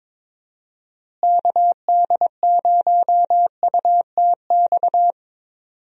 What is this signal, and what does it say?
Morse code, heard as a keyed tone, about 21 wpm, 700 Hz. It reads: KD0UTX
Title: USA Call Signs - 6 Characters - Repeated 2x - 22wpm